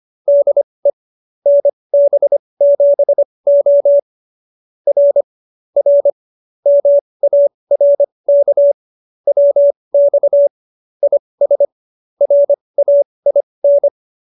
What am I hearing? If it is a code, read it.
Morse code, heard as a keyed tone, about 25 wpm, spaced as for 20 wpm, 570 Hz. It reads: DE NB7O R R MARK WX IS RAIN